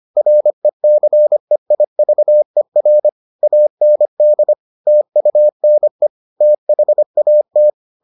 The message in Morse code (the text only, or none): RECEIVER AND TUNE THAT